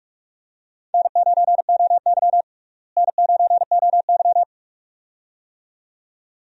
Morse code, heard as a keyed tone, about 46 wpm, 700 Hz. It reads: N9OY N9OY
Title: USA Call Signs - 4 Characters - Repeated 2x - 45wpm